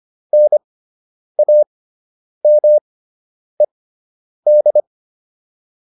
Morse code, heard as a keyed tone, about 25 wpm, spaced as for 9 wpm, 600 Hz. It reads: NAMED